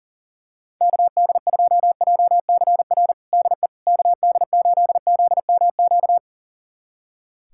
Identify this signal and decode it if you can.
Morse code, heard as a keyed tone, about 40 wpm, 700 Hz. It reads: KD2JCR DE KD8ZMQ
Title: USA Callsign DE Callsign 40wpm